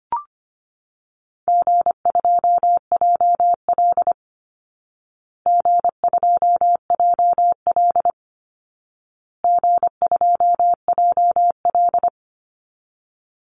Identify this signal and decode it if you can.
Morse code, heard as a keyed tone, about 25 wpm, 700 Hz. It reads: G2JL G2JL G2JL
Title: Easy Call Signs Repeated 3x - 25wpm